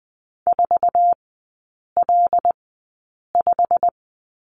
Morse code, heard as a keyed tone, about 20 wpm, spaced as for 8 wpm, 700 Hz. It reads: 4L5
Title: 087 Lesson 15e - Number 5 - Three Characters 20wpm